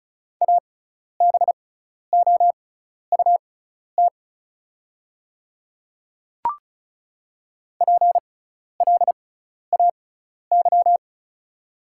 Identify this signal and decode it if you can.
Morse code, heard as a keyed tone, about 34 wpm, spaced as for 12 wpm, 690 Hz. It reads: ABOUT PLAY
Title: 6x Character Spacing - Top 500 Words - 35wpm